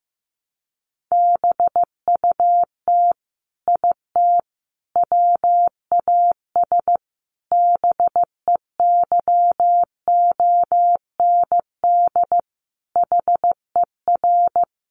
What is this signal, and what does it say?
Morse code, heard as a keyed tone, about 15 wpm, 700 Hz. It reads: BUT IT WAS BEYOND HER
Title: Sentences from Top 1200 Words 15wpm